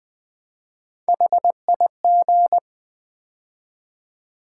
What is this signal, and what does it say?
Morse code, heard as a keyed tone, about 20 wpm, 700 Hz. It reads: HIG
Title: Build A Word - 3 Letters At A Time - 9-Letter Words - 20wpm